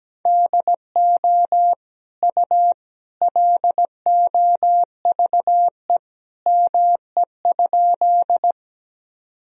Morse code, heard as a keyed tone, about 17 wpm, 700 Hz. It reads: DO U LOVE ME?
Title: Sentences from Top 400 Words and Abbreviations 17wpm